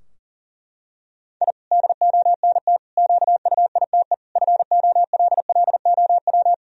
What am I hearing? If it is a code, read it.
Morse code, heard as a keyed tone, about 37 wpm, 700 Hz. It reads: I DONT QUITE FOLLOW
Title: Sentences from Top 1000 Words - Review Part 4 - 40wpm